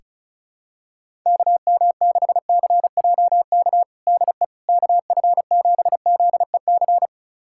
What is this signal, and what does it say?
Morse code, heard as a keyed tone, about 36 wpm, 700 Hz. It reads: KM6CJK DE KF7ZEC
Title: USA Callsign DE Callsign - 6 Characters - 35wpm